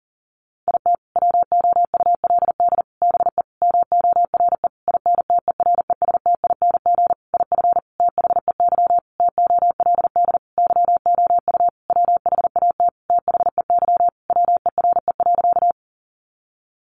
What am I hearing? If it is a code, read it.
Morse code, heard as a keyed tone, about 37 wpm, 700 Hz. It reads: IT WOULD BE MORE INTERESTING IF THEY TOLD YOU WHAT THEY WERE.
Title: Sentences from Top 1000 Words 40wpm